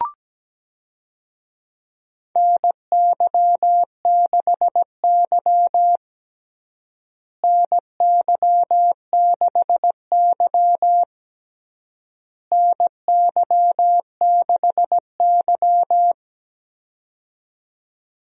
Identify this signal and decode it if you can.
Morse code, heard as a keyed tone, about 17 wpm, 700 Hz. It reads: NY6Y NY6Y NY6Y
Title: USA Call Signs - 4 Characters - Repeated 3x - 17wpm